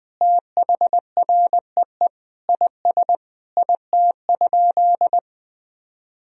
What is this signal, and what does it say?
Morse code, heard as a keyed tone, about 20 wpm, 700 Hz. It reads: THREE IS IT?
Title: Sentences from Top 1000 Words - Review Part 1 - 20wpm